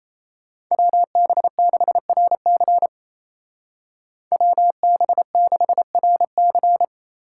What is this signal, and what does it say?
Morse code, heard as a keyed tone, about 35 wpm, 700 Hz. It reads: WB6RC WB6RC
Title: Speed Racing - USA Call Signs 28wpm